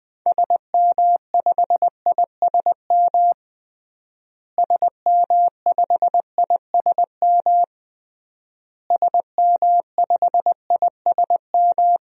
Morse code, heard as a keyed tone, about 21 wpm, 700 Hz. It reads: SM5ISM SM5ISM SM5ISM
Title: DXCC-100 Call Signs - 6 Characters - Repeated 3x - 20wpm